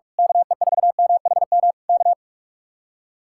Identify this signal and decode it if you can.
Morse code, heard as a keyed tone, about 46 wpm, 700 Hz. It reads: KE4MHM K
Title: Speed-Racing Common QSO Phrases Encore - 45wpm